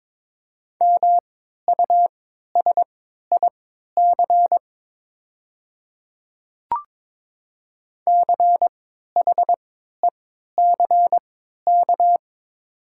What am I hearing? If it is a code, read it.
Morse code, heard as a keyed tone, about 22 wpm, spaced as for 13 wpm, 700 Hz. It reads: MUSIC CHECK
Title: 3x Character Spacing - Top 500 Words - 22wpm